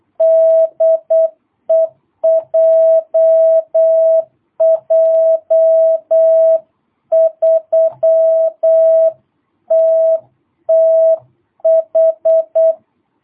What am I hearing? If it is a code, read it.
Morse code, heard as a keyed tone, about 8 wpm, 650 Hz. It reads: DE JJ3TTH
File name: jh8cbh / 2022cw